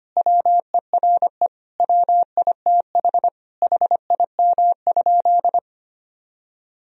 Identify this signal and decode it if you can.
Morse code, heard as a keyed tone, about 25 wpm, 700 Hz. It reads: WERE WITH HIM?
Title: Sentences from Top 1000 Words - Review Part 1 - 25wpm